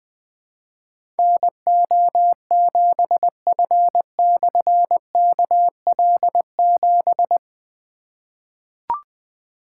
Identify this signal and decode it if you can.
Morse code, heard as a keyed tone, about 20 wpm, 700 Hz. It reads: NO7F/KL7
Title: Difficult Call Signs 20wpm